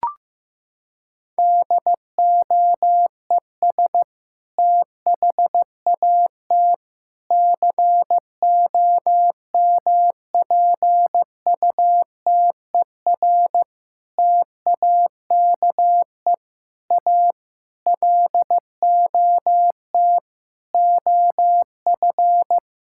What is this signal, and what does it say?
Morse code, heard as a keyed tone, about 15 wpm, 700 Hz. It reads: DOES THAT COMPUTER TAKE A LOT OF
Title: Sentences from Top 1000 Words - Review Part 6 - 15wpm